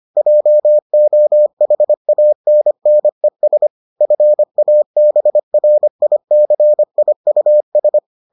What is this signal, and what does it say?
Morse code, heard as a keyed tone, about 25 wpm, 600 Hz. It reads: JOHANNES FABRICIUS